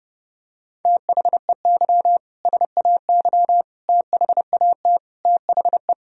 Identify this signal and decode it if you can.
Morse code, heard as a keyed tone, about 30 wpm, 700 Hz. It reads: THEY SAY THAT THE
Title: Sentences from Top 1000 Words - Review Part 3 - 30wpm